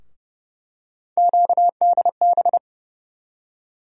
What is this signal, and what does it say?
Morse code, heard as a keyed tone, about 30 wpm, 700 Hz. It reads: QDB
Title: USA Call Sign Postfixes - 3 Letters 30wpm